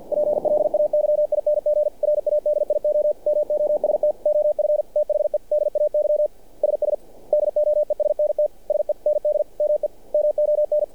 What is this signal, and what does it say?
Morse code, heard as a keyed tone, about 49 wpm, 600 Hz. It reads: PLAYING RADIO MOST OF THE DAY SS DOESNT SEND ME MON
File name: archived / CRSnet